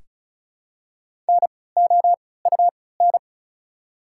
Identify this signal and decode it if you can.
Morse code, heard as a keyed tone, about 36 wpm, spaced as for 20 wpm, 690 Hz. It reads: NOUN
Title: 3x Character Spacing - Top 500 Words - 35wpm